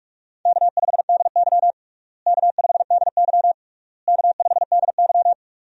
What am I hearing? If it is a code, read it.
Morse code, heard as a keyed tone, about 46 wpm, 700 Hz. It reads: K5DY K5DY K5DY
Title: USA Call Signs - 4 Characters - Repeated 3x - 45wpm